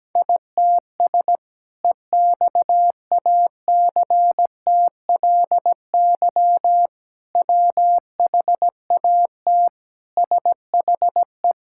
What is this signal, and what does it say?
Morse code, heard as a keyed tone, about 17 wpm, 700 Hz. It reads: ITS EXACTLY WHAT SHE
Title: Sentences from Top 1000 Words - Review Part 6 - 17wpm